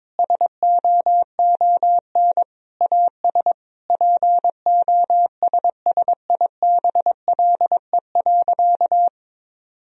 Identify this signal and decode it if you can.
Morse code, heard as a keyed tone, about 22 wpm, 700 Hz. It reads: SOON AS POSSIBLE.
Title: Sentences from Top 1000 Words - Review Part 2 - 22wpm